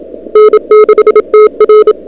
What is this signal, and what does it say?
Morse code, heard as a keyed tone, about 30 wpm, 440 Hz. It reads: N6TR
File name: n6tr / Fab2007CWSprintHour1